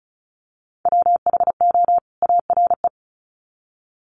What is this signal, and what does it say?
Morse code, heard as a keyed tone, about 34 wpm, 700 Hz. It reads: WHO ARE
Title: Sets of 2 Words 35wpm